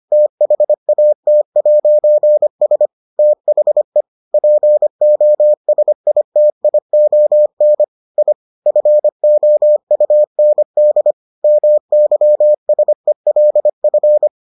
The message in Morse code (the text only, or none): THAT'S THE POSITION I FOUND MYSELF